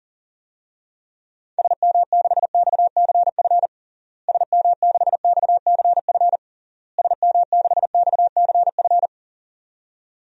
Morse code, heard as a keyed tone, about 40 wpm, 700 Hz. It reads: SM6XCF SM6XCF SM6XCF
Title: DXCC-100 Call Signs - 6 Characters - Repeated 3x - 40wpm